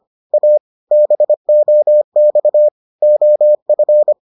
Morse code, heard as a keyed tone, about 25 wpm, 600 Hz. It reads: A BOX OF